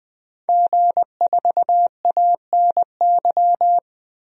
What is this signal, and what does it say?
Morse code, heard as a keyed tone, about 20 wpm, 700 Hz. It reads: G4ANY
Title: DXCC-100 Call Signs - 5 Characters - 20wpm